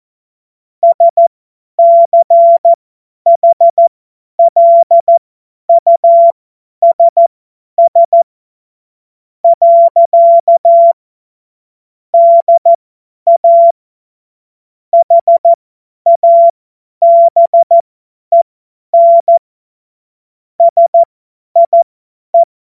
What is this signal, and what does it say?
Morse code, heard as a keyed tone, about 14 wpm, spaced as for 10 wpm, 680 Hz. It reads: SCHLUSS. DA HABEN SIE
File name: Morsekurs / Witze / Witz 02 Januar 2024 50 F70